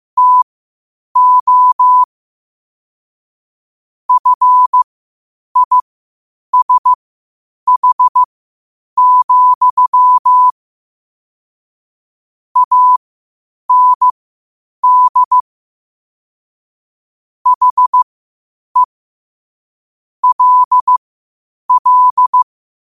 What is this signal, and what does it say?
Morse code, heard as a keyed tone, about 15 wpm, spaced as for 8 wpm, 1000 Hz. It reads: TO FISH, AND HE LL
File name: qotdfiles / 8wpm